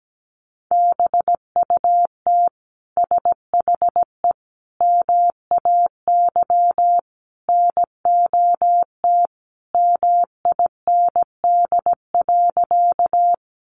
Morse code, heard as a keyed tone, about 17 wpm, 700 Hz. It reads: BUT SHE MAY NOT MIND.
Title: Sentences from Top 600 Words 17wpm